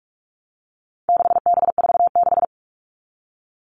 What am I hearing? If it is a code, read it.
Morse code, heard as a keyed tone, about 44 wpm, 700 Hz. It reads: 6B46
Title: All Cut Numbers - 4-Digit 45wpm